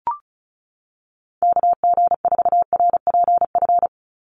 Morse code, heard as a keyed tone, about 34 wpm, 700 Hz. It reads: KG4RPF
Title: USA Call Signs 35wpm